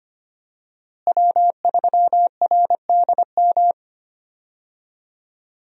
Morse code, heard as a keyed tone, about 25 wpm, 700 Hz. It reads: W3RDM
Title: USA Call Signs - 5 Characters - Repeated 2x - 25wpm